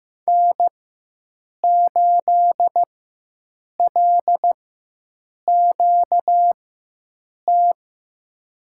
Morse code, heard as a keyed tone, about 15 wpm, spaced as for 7 wpm, 700 Hz. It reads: N8LQT
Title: USA Call Signs - 5 Characters - 4x Character Spacing - 15wpm